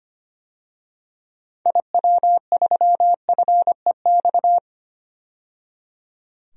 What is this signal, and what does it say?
Morse code, heard as a keyed tone, about 25 wpm, 700 Hz. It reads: IW3FEX
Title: DXCC-100 Call Signs - 6 Characters - 25wpm